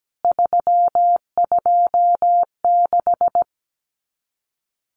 Morse code, heard as a keyed tone, about 17 wpm, 700 Hz. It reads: 326
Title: CWT Exchange 17wpm